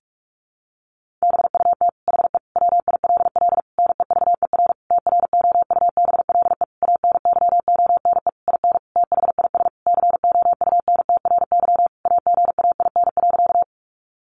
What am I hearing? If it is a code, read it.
Morse code, heard as a keyed tone, about 44 wpm, 700 Hz. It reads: BUT HE WILL NEVER TROUBLE ANYONE IN THIS COUNTRY AGAIN.
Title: Sentences from Top 1200 Words 45wpm